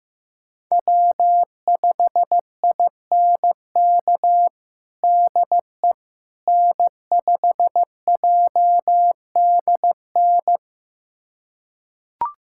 Morse code, heard as a keyed tone, about 15 wpm, 700 Hz. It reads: W5INK DE N5JDN
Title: USA Callsign DE Callsign - 5 Characters - 15wpm